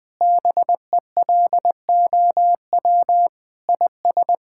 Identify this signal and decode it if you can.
Morse code, heard as a keyed tone, about 20 wpm, 700 Hz. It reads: BELOW IS
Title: Sentences from Top 1000 Words - Review Part 5 - 20wpm